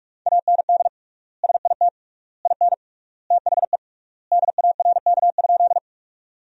Morse code, heard as a keyed tone, about 46 wpm, 700 Hz. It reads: AND SIT IN THE DARK?
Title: 3x Word-Spacing - Sentences from Top 500 Words - 45wpm